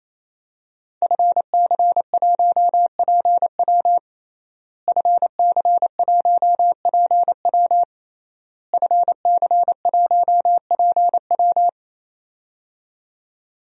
Morse code, heard as a keyed tone, about 27 wpm, 700 Hz. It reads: FC1PW FC1PW FC1PW
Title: DXCC-100 Call Signs - 5 Characters - Repeated 3x - 28wpm